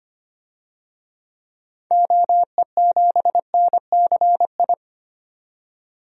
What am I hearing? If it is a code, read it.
Morse code, heard as a keyed tone, about 25 wpm, 700 Hz. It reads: OE7NCI
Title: DXCC-100 Call Signs - 6 Characters - 25wpm